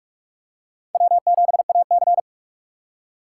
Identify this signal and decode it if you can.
Morse code, heard as a keyed tone, about 44 wpm, 700 Hz. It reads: W7AC